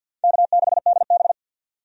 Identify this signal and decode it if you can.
Morse code, heard as a keyed tone, about 52 wpm, 700 Hz. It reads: K6DB
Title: USA Call Signs - 4 Characters - 50wpm